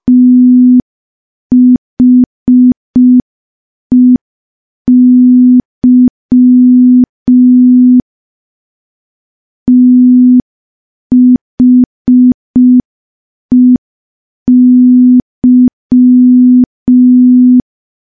Morse code, heard as a keyed tone, about 5 wpm, 250 Hz. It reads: THEY THEY
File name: morse / morse-training-05WPM